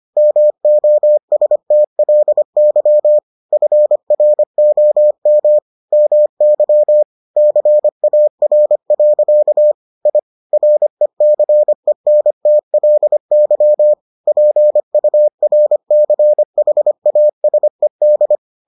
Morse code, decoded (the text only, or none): MOSTLY FROM MY CAR. I RECENTLY PURCHASED